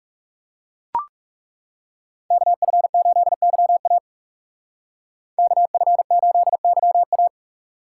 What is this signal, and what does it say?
Morse code, heard as a keyed tone, about 35 wpm, 700 Hz. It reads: KF8YA KF8YA
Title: Speed Racing - USA Call Signs 30wpm